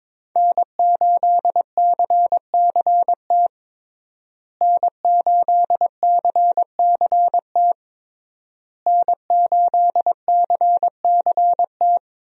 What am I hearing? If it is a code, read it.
Morse code, heard as a keyed tone, about 22 wpm, 700 Hz. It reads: N8CCT N8CCT N8CCT
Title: USA Call Signs - 5 Characters - Repeated 3x - 22wpm